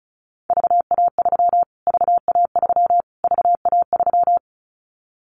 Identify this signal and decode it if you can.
Morse code, heard as a keyed tone, about 34 wpm, 700 Hz. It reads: VA3 VA3 VA3
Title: Callsign Prefixes for Top 25 DXCC - Repeated 3x - 35wpm